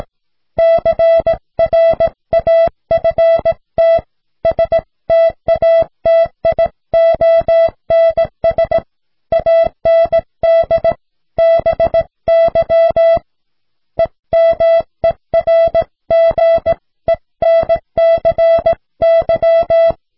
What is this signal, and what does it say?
Morse code, heard as a keyed tone, about 18 wpm, 660 Hz. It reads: CRAFT STATIONS AND BY EMERGENCY